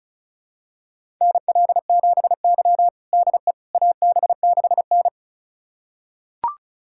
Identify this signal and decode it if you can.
Morse code, heard as a keyed tone, about 36 wpm, 700 Hz. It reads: NL7Y DE AB6N
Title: USA Callsign DE Callsign - 4 Characters - 35wpm